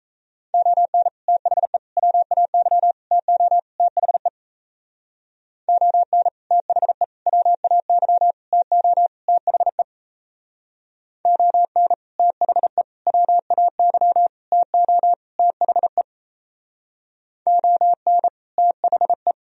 Morse code, heard as a keyed tone, about 34 wpm, 700 Hz. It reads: ON THE WAY TO THE ON THE WAY TO THE ON THE WAY TO THE ON THE
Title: Speed-Racing - Sets of 5 Words 28wpm